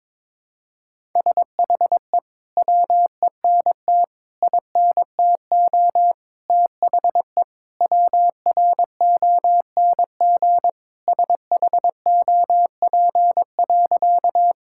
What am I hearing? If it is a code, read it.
Morse code, heard as a keyed tone, about 22 wpm, 700 Hz. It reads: SHE WENT INTO THE WRONG SHOP.
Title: Sentences from Top 1600 Words 22wpm